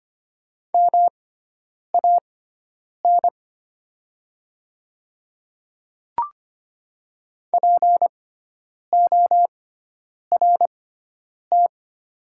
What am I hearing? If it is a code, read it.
Morse code, heard as a keyed tone, about 25 wpm, spaced as for 9 wpm, 690 Hz. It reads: MAN PORT
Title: 6x Character Spacing - Top 500 Words - 25wpm